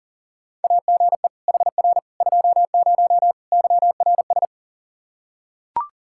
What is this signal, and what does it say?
Morse code, heard as a keyed tone, about 40 wpm, 700 Hz. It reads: AGE HR 20 YRS
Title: K1USN SST QSO Practice 40wpm